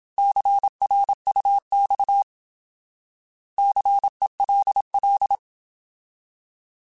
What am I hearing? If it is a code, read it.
Morse code, heard as a keyed tone, about 27 wpm, 800 Hz. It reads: CRUX CELL
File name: Alphabetical / Alphabetical mp3s / Cs @25